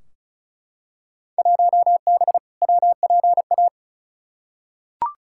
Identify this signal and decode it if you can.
Morse code, heard as a keyed tone, about 35 wpm, 690 Hz. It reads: 1B WPA
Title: Field Day Exchanges Encore - 35wpm